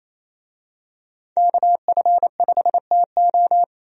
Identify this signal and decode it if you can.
Morse code, heard as a keyed tone, about 28 wpm, 700 Hz. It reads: KF5TO
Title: USA Call Signs - 5 Characters - 28wpm